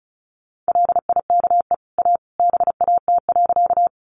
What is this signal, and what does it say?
Morse code, heard as a keyed tone, about 34 wpm, 700 Hz. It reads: LIKE A BAT.
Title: Sentences from Top 1500 Words 35wpm